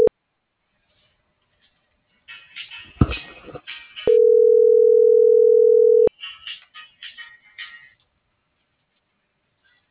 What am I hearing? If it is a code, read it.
Ringback tone (North America), 440 + 480 Hz.